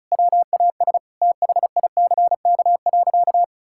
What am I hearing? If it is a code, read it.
Morse code, heard as a keyed tone, about 36 wpm, 700 Hz. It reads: WAS THICK.
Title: Sentences from Top 1600 Words 35wpm